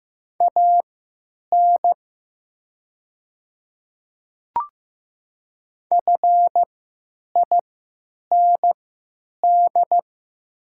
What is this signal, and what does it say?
Morse code, heard as a keyed tone, about 15 wpm, spaced as for 9 wpm, 700 Hz. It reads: AN FIND
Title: 3x Character Spacing - Top 300 Words - 15wpm